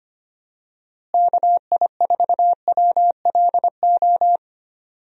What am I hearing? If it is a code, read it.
Morse code, heard as a keyed tone, about 25 wpm, 700 Hz. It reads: KI4WLO
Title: USA Call Signs - 6 Characters - Repeated 2x - 25wpm